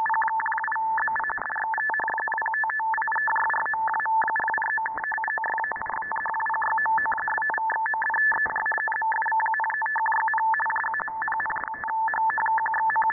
Two-tone teleprinter signal, about 850 Hz shift.